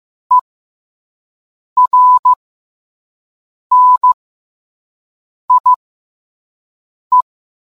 Morse code, heard as a keyed tone, about 15 wpm, spaced as for 5 wpm, 1000 Hz. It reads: ERNIE